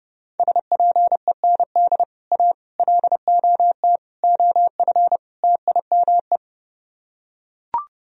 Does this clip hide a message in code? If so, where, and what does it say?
Yes: Morse code from 0.4 to 6.4 s, heard as a keyed tone, about 30 wpm, 700 Hz. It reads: SPEND A LOT OF TIME